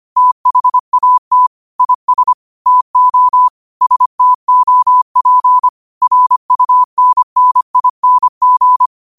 Morse code, heard as a keyed tone, about 25 wpm, 1000 Hz. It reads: THAT IS TO STOP RUNNING
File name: qotdfiles / 25wpm